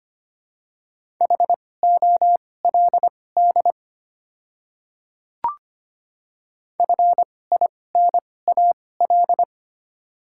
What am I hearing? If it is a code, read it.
Morse code, heard as a keyed tone, about 25 wpm, spaced as for 18 wpm, 700 Hz. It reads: HOLD FINAL
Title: 2x Character Spacing - Top 500 Words - 25wpm